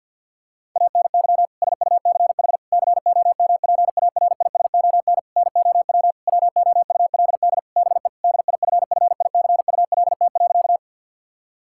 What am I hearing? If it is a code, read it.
Morse code, heard as a keyed tone, about 52 wpm, 700 Hz. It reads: ANY SUCH COMPARISON NOW WOULD BE DIFFICULT.
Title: Sentences from Top 1600 Words 50wpm